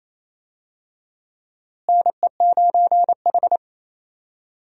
Morse code, heard as a keyed tone, about 28 wpm, 700 Hz. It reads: NE9H
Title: USA Call Signs - 4 Characters - Repeated 3x - 28wpm